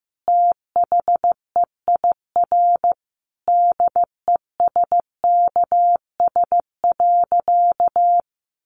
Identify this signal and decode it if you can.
Morse code, heard as a keyed tone, about 15 wpm, 700 Hz. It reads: THEIR DESKS.
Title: Additional Easy Phrases 15wpm